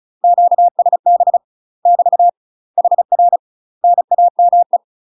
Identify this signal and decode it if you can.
Morse code, heard as a keyed tone, about 35 wpm, 700 Hz. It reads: QSB <BT> HR NAME
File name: UK MASTER QSOs TRAINING / 35WPM / UK MASTER QSOs 0010